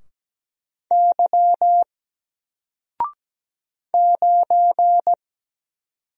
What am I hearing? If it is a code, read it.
Morse code, heard as a keyed tone, about 17 wpm, 700 Hz. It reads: Y 9
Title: Single Character (Letter:Number:Punctuation:Prosign) 17wpm